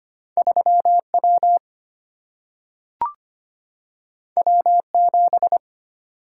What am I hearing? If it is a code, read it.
Morse code, heard as a keyed tone, about 25 wpm, 700 Hz. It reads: 3 W W 7
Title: Single-Letter Single-Number 25wpm